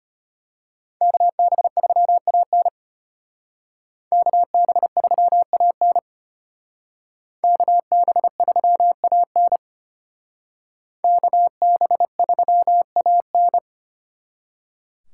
Morse code, heard as a keyed tone, about 32 wpm, 700 Hz. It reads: KB3AN KB3AN KB3AN KB3AN